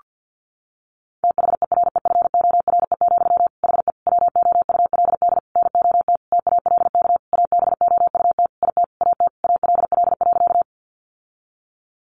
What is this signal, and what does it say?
Morse code, heard as a keyed tone, about 50 wpm, 700 Hz. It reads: THEREFORE, HE WOULD NOT TALK ABOUT IT AT ALL.
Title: Sentences from Top 800 Words 50wpm